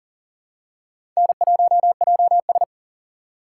Morse code, heard as a keyed tone, about 40 wpm, 700 Hz. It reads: N1JS
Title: USA Call Signs - 4 Characters - 3x Character Spacing - 40wpm